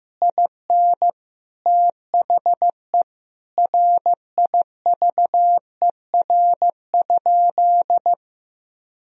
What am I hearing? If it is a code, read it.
Morse code, heard as a keyed tone, about 16 wpm, 700 Hz. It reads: IN THE RIVER?
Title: Sentences from Top 1000 Words - Review Part 4 - 15wpm